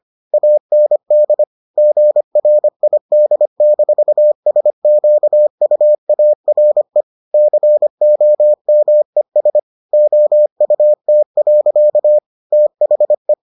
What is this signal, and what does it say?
Morse code, heard as a keyed tone, about 25 wpm, 600 Hz. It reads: AND GRID-SQUARE COMES OUT. THE